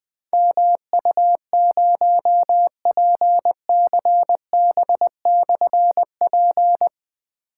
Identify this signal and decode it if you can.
Morse code, heard as a keyed tone, about 20 wpm, 700 Hz. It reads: MU0PCB/P
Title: Difficult Call Signs 20wpm